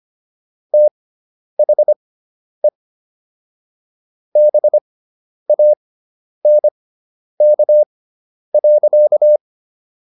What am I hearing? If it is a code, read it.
Morse code, heard as a keyed tone, about 25 wpm, spaced as for 10 wpm, 600 Hz. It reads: THE BANK.